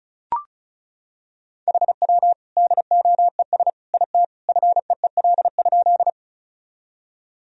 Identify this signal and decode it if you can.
Morse code, heard as a keyed tone, about 36 wpm, 700 Hz. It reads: HW DOES IT FEEL?
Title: Sentences from Top 400 Words and Abbreviations 35wpm